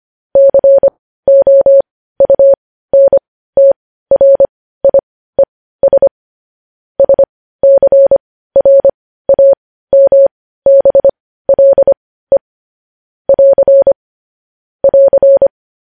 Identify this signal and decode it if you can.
Morse code, heard as a keyed tone, about 25 wpm, spaced as for 15 wpm, 550 Hz. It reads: COUNTRIES SCRAMBLE <AR> <AR>
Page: news-headlines-in-morse-code-at-15-wpm